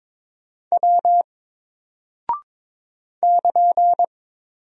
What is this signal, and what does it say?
Morse code, heard as a keyed tone, about 22 wpm, 700 Hz. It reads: W <KN>
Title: Single Character (Letter:Number:Punctuation:Prosign) 22wpm